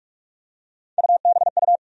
Morse code, heard as a keyed tone, about 46 wpm, 700 Hz. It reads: UBU
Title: USA Call Sign Postfixes - 3 Letters 45wpm